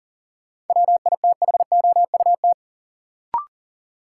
Morse code, heard as a keyed tone, about 40 wpm, 700 Hz. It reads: WITHOUT
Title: Sentences from Top 600 Words - New Word Drill - 40wpm